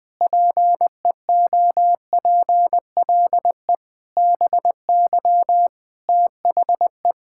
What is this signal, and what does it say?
Morse code, heard as a keyed tone, about 20 wpm, 700 Hz. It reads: PEOPLE BY THE
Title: Sentences with 8-Word Phrases - 20wpm